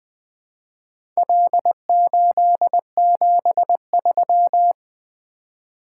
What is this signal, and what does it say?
Morse code, heard as a keyed tone, about 20 wpm, 700 Hz. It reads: L873
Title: Four Letter-Number Combinations - 20wpm